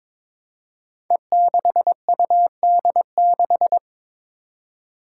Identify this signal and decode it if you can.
Morse code, heard as a keyed tone, about 22 wpm, 700 Hz. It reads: E6UD6
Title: Five Challenging Letter-Number Combinations 22wpm